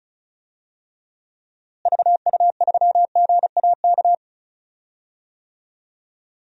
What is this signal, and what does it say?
Morse code, heard as a keyed tone, about 36 wpm, 700 Hz. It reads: VU3GAK
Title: DXCC-100 Call Signs - 6 Characters - Repeated 2x - 35wpm